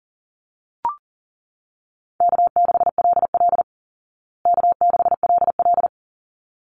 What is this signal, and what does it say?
Morse code, heard as a keyed tone, about 40 wpm, 700 Hz. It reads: K6LL K6LL
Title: Easy Call Signs Repeated 2x - 40wpm